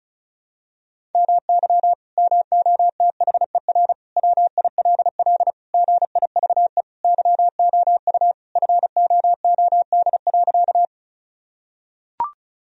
Morse code, heard as a keyed tone, about 34 wpm, 700 Hz. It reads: MY MOTHER WILL GIVE YOU FOOD.
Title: Sentences from Top 1000 Words - Review Part 1 - 35wpm